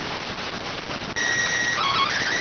SSTV (slow-scan television), PD 290.